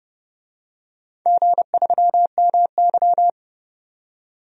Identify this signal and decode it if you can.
Morse code, heard as a keyed tone, about 29 wpm, 700 Hz. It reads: G3MY
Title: DXCC-100 Call Signs - 4 Characters - 30wpm